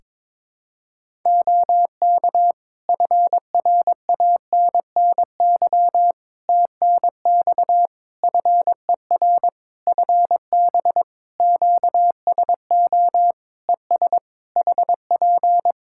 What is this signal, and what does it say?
Morse code, heard as a keyed tone, about 22 wpm, 700 Hz. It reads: OK FRANNY TNX FER FB QSO ES HP
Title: Practice QSO - Exchange 4 - Ending - 22wpm